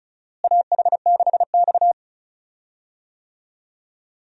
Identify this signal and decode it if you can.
Morse code, heard as a keyed tone, about 34 wpm, 700 Hz. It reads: AH6X
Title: USA Call Signs - 4 Characters - Repeated 2x - 35wpm